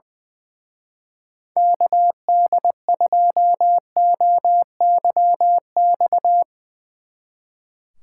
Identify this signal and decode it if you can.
Morse code, heard as a keyed tone, about 22 wpm, 700 Hz. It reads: KD2OYX
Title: Speed Racing - USA Call Signs 20wpm